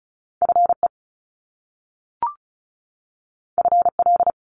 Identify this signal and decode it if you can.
Morse code, heard as a keyed tone, about 36 wpm, 700 Hz. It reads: FE FL
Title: Top Two Letter Combinations in Words 35wpm